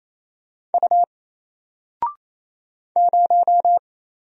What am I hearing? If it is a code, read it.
Morse code, heard as a keyed tone, about 27 wpm, 700 Hz. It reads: U 0
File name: Single-Letter-Number-28wpm